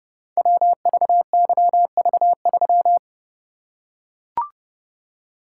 Morse code, heard as a keyed tone, about 31 wpm, 700 Hz. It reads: WVYV3
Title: Code Groups - 5 Letter:Numbers - 30wpm